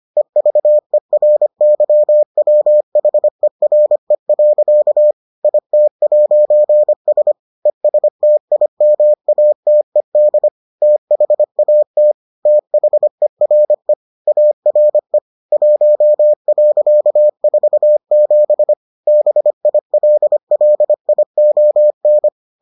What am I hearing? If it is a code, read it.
Morse code, heard as a keyed tone, about 25 wpm, 600 Hz. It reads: EVERYWHERE. IT'S ESTIMATED THAT THERE ARE 1.47 BILLION